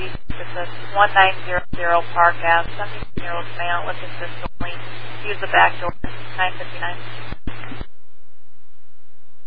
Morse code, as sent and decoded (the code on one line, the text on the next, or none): none